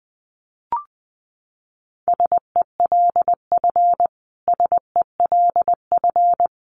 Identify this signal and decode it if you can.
Morse code, heard as a keyed tone, about 20 wpm, 700 Hz. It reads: SELF SELF
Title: Top 200 Words Repeated 20wpm